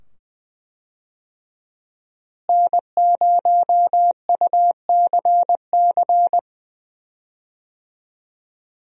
Morse code, heard as a keyed tone, about 21 wpm, 700 Hz. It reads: N0UCC
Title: USA Call Signs - 5 Characters - Repeated 2x - 20wpm